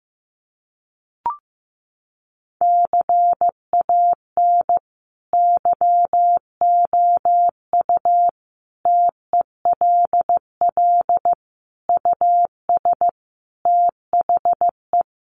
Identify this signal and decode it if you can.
Morse code, heard as a keyed tone, about 15 wpm, 700 Hz. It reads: CAN YOU TELL US THE
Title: Sentences from Top 1300 Words 15wpm